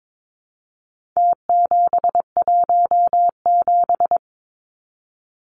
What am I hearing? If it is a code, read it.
Morse code, heard as a keyed tone, about 22 wpm, 700 Hz. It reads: T717
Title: Numbers - Contest Style - 4-Digit 22wpm